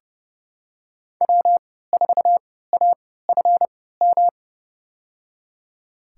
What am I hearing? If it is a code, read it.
Morse code, heard as a keyed tone, about 30 wpm, spaced as for 17 wpm, 700 Hz. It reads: W4AFM
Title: USA Call Signs - 5 Characters - 3x Character Spacing - 30wpm